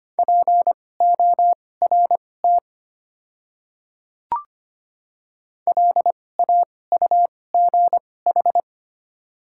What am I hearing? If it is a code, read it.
Morse code, heard as a keyed tone, about 25 wpm, spaced as for 18 wpm, 700 Hz. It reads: PORT LAUGH
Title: 2x Character Spacing - Top 500 Words - 25wpm